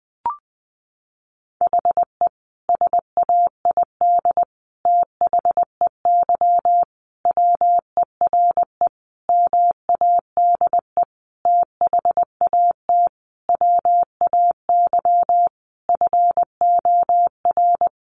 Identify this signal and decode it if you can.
Morse code, heard as a keyed tone, about 20 wpm, 700 Hz. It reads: HE SAID THEY WERE MADE THAT WAY FOR
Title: Sentences from Top 700 Words 20wpm